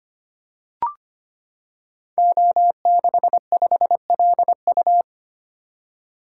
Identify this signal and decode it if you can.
Morse code, heard as a keyed tone, about 25 wpm, 700 Hz. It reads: O65LU
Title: Code Groups - 5 Letter:Numbers - 25wpm